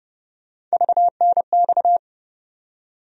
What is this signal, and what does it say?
Morse code, heard as a keyed tone, about 29 wpm, 700 Hz. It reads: VNX
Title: USA Call Sign Postfixes - 3 Letters 30wpm